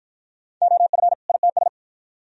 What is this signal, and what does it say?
Morse code, heard as a keyed tone, about 48 wpm, 700 Hz. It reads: OF ITS